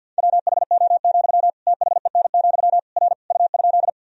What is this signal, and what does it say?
Morse code, heard as a keyed tone, about 48 wpm, 700 Hz. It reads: WHO, THEN, R U?
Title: Sentences from Top 100 Words and Abbreviations 50wpm